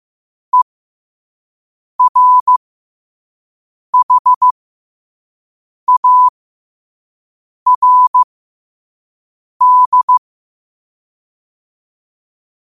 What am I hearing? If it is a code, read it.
Morse code, heard as a keyed tone, about 15 wpm, spaced as for 5 wpm, 1000 Hz. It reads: ERHARD
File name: qotdfiles / 5wpm